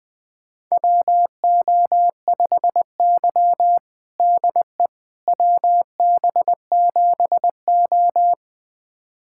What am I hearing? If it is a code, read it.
Morse code, heard as a keyed tone, about 21 wpm, 700 Hz. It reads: WO5Y DE WB7O
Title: USA Callsign DE Callsign - 4 Characters - 20wpm